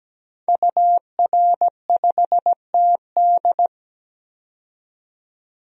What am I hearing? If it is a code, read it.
Morse code, heard as a keyed tone, about 17 wpm, 700 Hz. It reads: UR5TD
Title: DXCC-100 Call Signs - 5 Characters - 17wpm